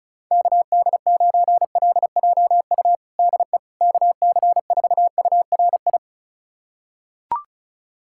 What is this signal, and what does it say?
Morse code, heard as a keyed tone, about 36 wpm, 700 Hz. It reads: KD9LJU DE KC4URI
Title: USA Callsign DE Callsign 35wpm